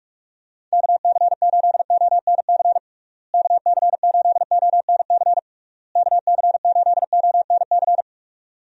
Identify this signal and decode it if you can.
Morse code, heard as a keyed tone, about 46 wpm, 700 Hz. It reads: KC8ONC KC8ONC KC8ONC
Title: USA Call Signs - 6 Characters - Repeated 3x - 45wpm